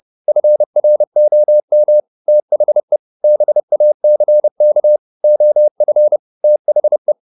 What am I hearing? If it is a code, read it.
Morse code, heard as a keyed tone, about 30 wpm, 600 Hz. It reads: FROM THE BACK OF THE
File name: the old man and the sea / the old man and the sea 1 30wpm